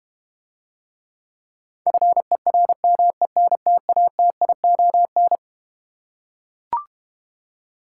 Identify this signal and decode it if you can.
Morse code, heard as a keyed tone, about 32 wpm, 690 Hz. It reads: FERMENTATION